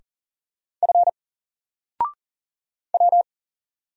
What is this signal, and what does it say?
Morse code, heard as a keyed tone, about 40 wpm, 700 Hz. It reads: F W
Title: Single Character (Letter:Number:Punctuation:Prosign) 40wpm